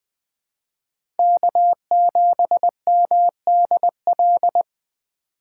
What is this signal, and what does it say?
Morse code, heard as a keyed tone, about 20 wpm, 700 Hz. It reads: K7MDL
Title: USA Call Signs - 5 Characters - 3x Character Spacing - 20wpm